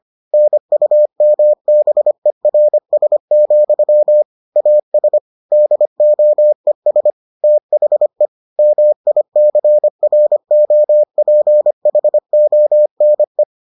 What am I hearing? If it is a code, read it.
Morse code, heard as a keyed tone, about 25 wpm, 600 Hz. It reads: NUMBERS, AS DOES THE MICROPHONE